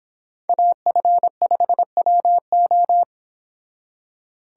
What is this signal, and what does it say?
Morse code, heard as a keyed tone, about 26 wpm, 700 Hz. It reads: AF5WO